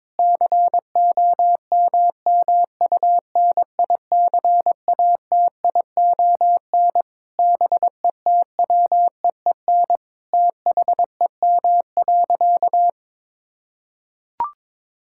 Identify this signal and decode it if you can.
Morse code, heard as a keyed tone, about 22 wpm, 700 Hz. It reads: COMMUNICATION BETWEEN THEM.
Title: Sentences from Top 1000 Words - Review Part 6 - 22wpm